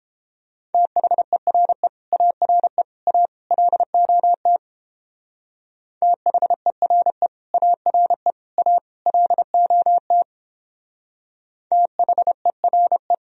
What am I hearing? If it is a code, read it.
Morse code, heard as a keyed tone, about 29 wpm, 700 Hz. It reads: THERE ARE A LOT THERE ARE A LOT THERE